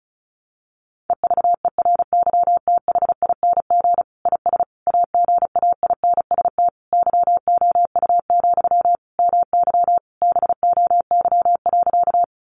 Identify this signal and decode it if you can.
Morse code, heard as a keyed tone, about 34 wpm, 700 Hz. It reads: EVERYTHING IS AGAINST YOU, MY BOY.
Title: Sentences from Top 700 Words 35wpm